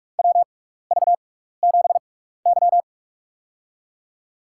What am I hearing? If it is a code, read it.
Morse code, heard as a keyed tone, about 46 wpm, spaced as for 15 wpm, 690 Hz. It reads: WV7Y
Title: USA Call Signs - 6x Character Spacing - 45wpm